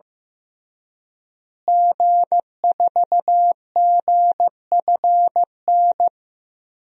Morse code, heard as a keyed tone, about 15 wpm, 700 Hz. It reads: G4GFN
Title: DXCC-100 Call Signs - 5 Characters - Repeated 3x - 15wpm